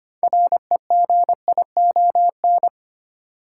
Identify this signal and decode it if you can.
Morse code, heard as a keyed tone, about 25 wpm, 700 Hz. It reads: REGION
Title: Sentences from Top 1500 Words - New Word Drill Alt - 25wpm